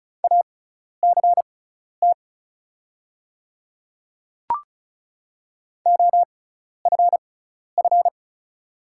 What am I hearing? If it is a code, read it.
Morse code, heard as a keyed tone, about 34 wpm, spaced as for 12 wpm, 690 Hz. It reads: ACT OFF
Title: 6x Character Spacing - Top 300 Words - 35wpm